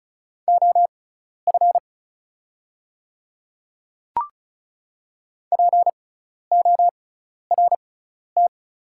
Morse code, heard as a keyed tone, about 34 wpm, spaced as for 12 wpm, 700 Hz. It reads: OF PORT
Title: 6x Character Spacing - Top 200 Words - 35wpm